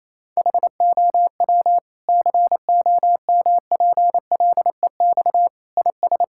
Morse code, heard as a keyed tone, about 28 wpm, 700 Hz. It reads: HOW COMPLEX IS
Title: Sentences from Top 1000 Words - Review Part 6 - 28wpm